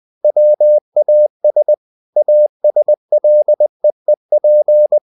Morse code, heard as a keyed tone, about 20 wpm, 600 Hz. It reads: WAS ASLEEP
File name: the old man and the sea part 2 20wpm